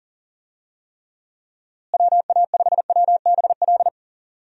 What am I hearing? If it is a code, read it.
Morse code, heard as a keyed tone, about 40 wpm, 700 Hz. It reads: WA5WBL
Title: USA Call Signs - 6 Characters - Repeated 3x - 40wpm